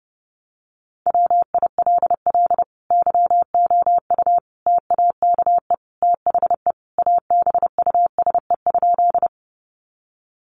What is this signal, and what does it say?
Morse code, heard as a keyed tone, about 29 wpm, 700 Hz. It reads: WILL YOU TAKE THE ABUSE?
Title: Sentences from Top 1300 Words 30wpm